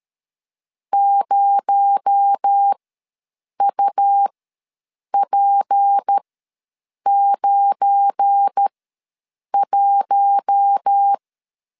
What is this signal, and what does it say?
Morse code, heard as a keyed tone, about 13 wpm, spaced as for 7 wpm, 800 Hz. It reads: 0UP91
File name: LEZIONI / 7 GF9Z / 7 GF9 ws 12wpm 3344